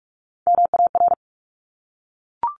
Morse code, heard as a keyed tone, about 46 wpm, 700 Hz. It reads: MAR
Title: Top Three Letter Combinations in Words 45wpm